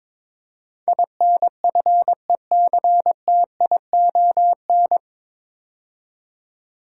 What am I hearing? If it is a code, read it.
Morse code, heard as a keyed tone, about 22 wpm, 700 Hz. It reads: INFECTION
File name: 9 Letter Words - First 3 Letters Given - 22wpm